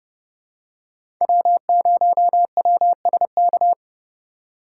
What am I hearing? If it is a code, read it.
Morse code, heard as a keyed tone, about 30 wpm, 700 Hz. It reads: W0WSK
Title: USA Call Signs - 3x Character Spacing - 30wpm